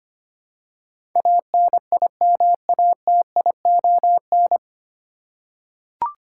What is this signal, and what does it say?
Morse code, heard as a keyed tone, about 25 wpm, 700 Hz. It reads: ANIMATION
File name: Nine Letter Words - 25wpm